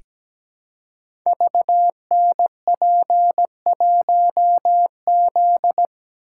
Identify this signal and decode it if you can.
Morse code, heard as a keyed tone, about 17 wpm, 700 Hz. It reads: VNP1Z
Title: Code Groups - 5 Letter:Numbers - 17wpm